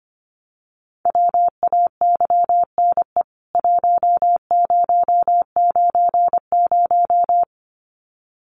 Morse code, heard as a keyed tone, about 25 wpm, 700 Hz. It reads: WAYNE 1090
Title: CWT Exchange 25wpm